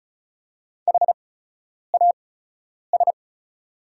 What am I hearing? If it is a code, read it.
Morse code, heard as a keyed tone, about 36 wpm, spaced as for 10 wpm, 700 Hz. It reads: HAS
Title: 8x Character Spacing - Top 100 Words - 35wpm